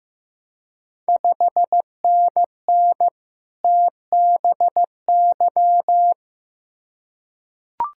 Morse code, heard as a keyed tone, about 15 wpm, 700 Hz. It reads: 5NN TBY
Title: Ontario QSO Party Exchange 15wpm